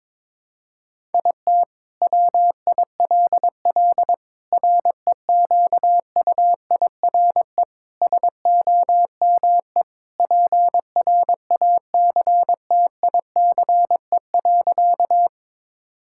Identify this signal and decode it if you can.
Morse code, heard as a keyed tone, about 22 wpm, 700 Hz. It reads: IT WILL REQUIRE SOME PRACTICE.
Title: Sentences from Top 1000 Words - Review Part 6 - 22wpm